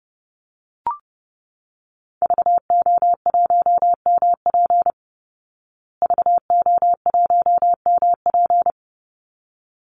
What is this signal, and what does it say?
Morse code, heard as a keyed tone, about 30 wpm, 700 Hz. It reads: VO1MP VO1MP
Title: Call Signs Repeated 3x - 30wpm